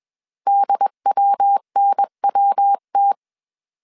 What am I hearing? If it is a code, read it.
Morse code, heard as a keyed tone, about 21 wpm, 800 Hz. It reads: DWNWT